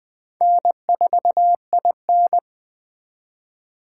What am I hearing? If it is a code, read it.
Morse code, heard as a keyed tone, about 20 wpm, 700 Hz. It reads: N4IN
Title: Speed Racing - USA 1x2 Call Signs 17wpm